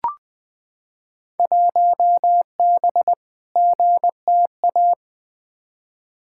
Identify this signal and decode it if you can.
Morse code, heard as a keyed tone, about 20 wpm, 700 Hz. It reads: 1B GTA
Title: Field Day Exchanges Encore - 20wpm